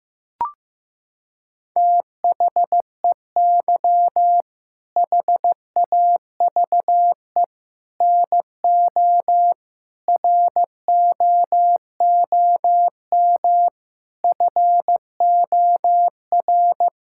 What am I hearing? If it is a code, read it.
Morse code, heard as a keyed tone, about 15 wpm, 700 Hz. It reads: THEY HAVE NO ROOM FOR
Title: Sentences from Top 1000 Words - Review Part 5 - 15wpm